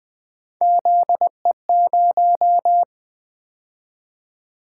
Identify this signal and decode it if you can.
Morse code, heard as a keyed tone, about 20 wpm, 700 Hz. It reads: ZE0
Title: Three Letter-Number Combinations - 20wpm